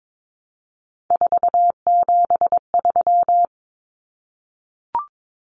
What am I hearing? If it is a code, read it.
Morse code, heard as a keyed tone, about 22 wpm, 700 Hz. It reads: 473
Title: Numbers - 3-Digit 22wpm